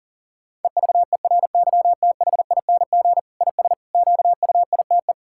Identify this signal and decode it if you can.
Morse code, heard as a keyed tone, about 37 wpm, 700 Hz. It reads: EVERYTHING IS QUITE
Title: Sentences from Top 1000 Words - Review Part 6 - 40wpm